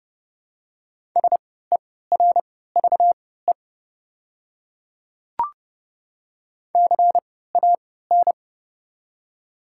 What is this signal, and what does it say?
Morse code, heard as a keyed tone, about 30 wpm, spaced as for 17 wpm, 700 Hz. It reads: SERVE CAN
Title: 3x Character Spacing - Top 500 Words - 30wpm